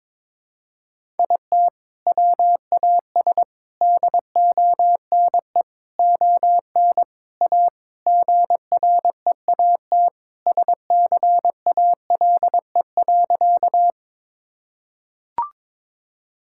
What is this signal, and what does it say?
Morse code, heard as a keyed tone, about 21 wpm, 700 Hz. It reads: IT WAS DONE ON A GREAT SCALE.
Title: Sentences from Top 1000 Words - Review Part 5 - 22wpm